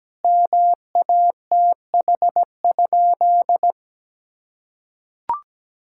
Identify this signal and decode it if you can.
Morse code, heard as a keyed tone, about 17 wpm, 700 Hz. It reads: MATH?
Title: Sentences from Top 1000 Words - Review Part 6 - 17wpm